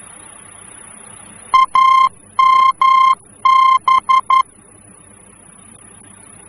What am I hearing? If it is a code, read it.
Morse code, heard as a keyed tone, about 11 wpm, 1050 Hz. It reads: AMB
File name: france / mx / AMB-F-VOR